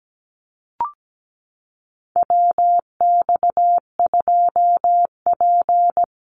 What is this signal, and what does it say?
Morse code, heard as a keyed tone, about 17 wpm, 700 Hz. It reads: WX2P
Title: USA Call Signs Repeated 3x 17wpm